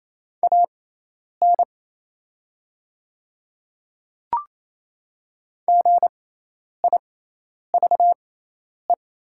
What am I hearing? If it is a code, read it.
Morse code, heard as a keyed tone, about 28 wpm, spaced as for 10 wpm, 700 Hz. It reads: AN GIVE
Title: 6x Character Spacing - Top 200 Words - 28wpm